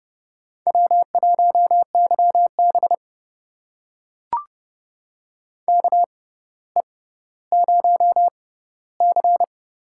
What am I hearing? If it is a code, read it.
Morse code, heard as a keyed tone, about 30 wpm, spaced as for 15 wpm, 700 Hz. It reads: W1YB KE0C
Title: USA Call Signs - 6x Character Spacing - 30wpm